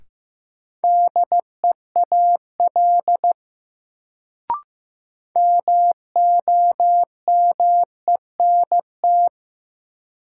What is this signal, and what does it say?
Morse code, heard as a keyed tone, about 16 wpm, 700 Hz. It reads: DEAL MOMENT
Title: Sets of 3 Words - New Word Drill - 15wpm